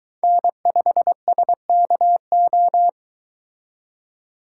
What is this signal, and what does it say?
Morse code, heard as a keyed tone, about 23 wpm, 700 Hz. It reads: N5SKO